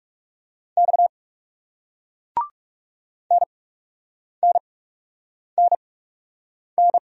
Morse code, heard as a keyed tone, about 39 wpm, spaced as for 8 wpm, 700 Hz. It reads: X NNNN